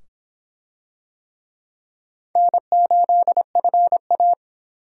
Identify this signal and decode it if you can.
Morse code, heard as a keyed tone, about 25 wpm, 700 Hz. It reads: N8FA